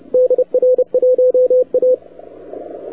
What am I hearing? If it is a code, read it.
Morse code, heard as a keyed tone, about 34 wpm, 500 Hz, with radio noise. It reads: DR1A